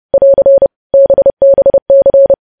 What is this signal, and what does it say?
Morse code, heard as a keyed tone, about 30 wpm, 550 Hz. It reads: <AR> BBC
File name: podcast / News headlines-1x-30x30